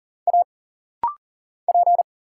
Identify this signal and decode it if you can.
Morse code, heard as a keyed tone, about 40 wpm, 700 Hz. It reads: A P
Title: Single Letter-Number Alt - Rapid-Fire 40wpm